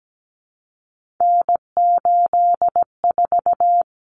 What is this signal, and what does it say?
Morse code, heard as a keyed tone, about 17 wpm, 700 Hz. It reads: N84
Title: Common Cut Numbers - 3-Digit 17wpm